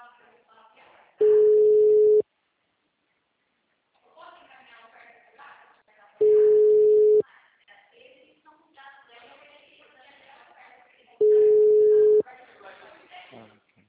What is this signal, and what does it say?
Ringback tone (Europe, 425 Hz), with speech.